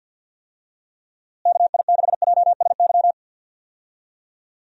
Morse code, heard as a keyed tone, about 50 wpm, 700 Hz. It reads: KI6JSY
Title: USA Call Signs - 6 Characters - Repeated 2x - 50wpm